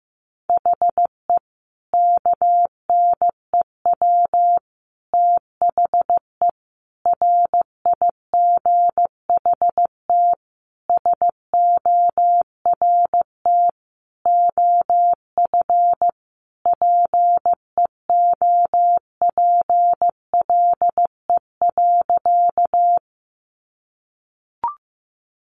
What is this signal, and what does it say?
Morse code, heard as a keyed tone, about 15 wpm, 700 Hz. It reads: HE KNEW THE RIGHT SORT OF PEOPLE.
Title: Sentences from Top 1300 Words 15wpm